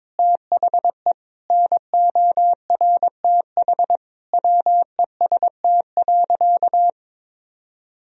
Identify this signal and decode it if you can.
Morse code, heard as a keyed tone, about 22 wpm, 700 Hz. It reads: THE NORTH WEST.
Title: Sentences from Top 1000 Words - Review Part 2 - 22wpm